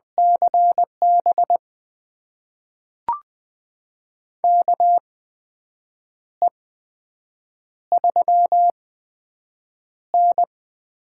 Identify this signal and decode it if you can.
Morse code, heard as a keyed tone, about 20 wpm, spaced as for 7 wpm, 700 Hz. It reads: CB KE3N
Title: USA Call Signs - 4 Characters - 8x Character Spacing - 20wpm